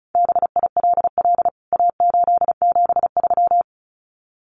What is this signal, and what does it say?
Morse code, heard as a keyed tone, about 34 wpm, 700 Hz. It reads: BILL A873
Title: CWT Exchange 35wpm